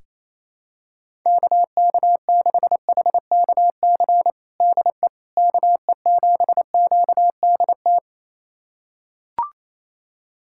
Morse code, heard as a keyed tone, about 28 wpm, 700 Hz. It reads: KK6HKC DE KE7QDT
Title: USA Callsign DE Callsign - 6 Characters - 28wpm